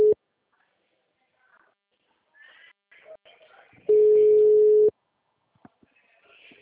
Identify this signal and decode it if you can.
Ringback tone (Europe, 425 Hz).
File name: out- -2015- - - .105